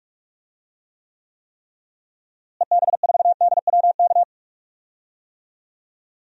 Morse code, heard as a keyed tone, about 46 wpm, 700 Hz. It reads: EB4DWK
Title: DXCC-100 Call Signs - 6 Characters - Repeated 3x - 45wpm